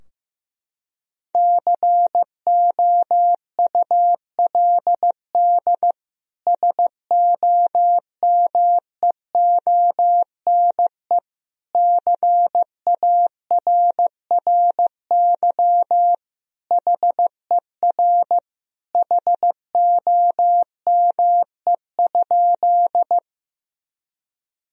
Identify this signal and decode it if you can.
Morse code, heard as a keyed tone, about 16 wpm, 700 Hz. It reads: COULD SOMEONE CARRY HER HOME?
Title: Sentences from Top 1000 Words - Review Part 4 - 15wpm